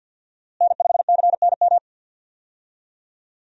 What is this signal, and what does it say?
Morse code, heard as a keyed tone, about 51 wpm, 700 Hz. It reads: N5CNM
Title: USA Call Signs - 5 Characters - Repeated 2x - 50wpm